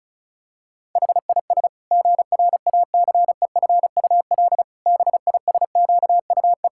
Morse code, heard as a keyed tone, about 34 wpm, 700 Hz. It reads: HIS GRACEFUL BISQUE
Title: Pangram Challenge v2 - 35wpm